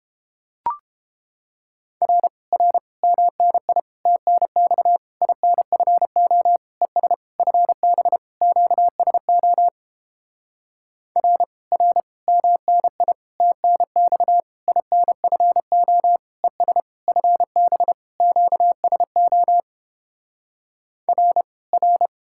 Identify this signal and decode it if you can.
Morse code, heard as a keyed tone, about 31 wpm, 700 Hz. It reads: R R MNI TNX INFO ES FB QSO R R MNI TNX INFO ES FB QSO R R